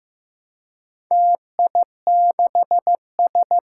Morse code, heard as a keyed tone, about 15 wpm, 700 Hz. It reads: TI6S
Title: Four Challenging Letter-Number Combinations 15wpm